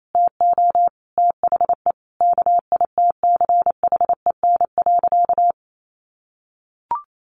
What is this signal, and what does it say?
Morse code, heard as a keyed tone, about 28 wpm, 700 Hz. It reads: TO THE KITCHEN.
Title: Sentences from Top 1400 Words 28wpm